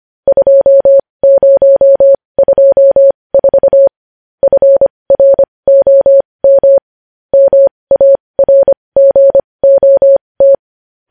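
Morse code, heard as a keyed tone, about 25 wpm, spaced as for 20 wpm, 550 Hz. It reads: 2024 FROM MARGOT
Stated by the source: Morse code news headlines practice at 20.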